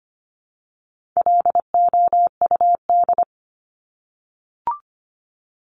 Morse code, heard as a keyed tone, about 25 wpm, 700 Hz. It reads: LOUD
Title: Top 1751-2000 Words 25wpm